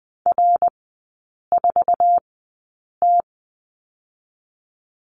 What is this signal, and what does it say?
Morse code, heard as a keyed tone, about 20 wpm, spaced as for 8 wpm, 700 Hz. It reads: R4T
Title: 093 Lesson 16e - Letter c - Three Characters 20wpm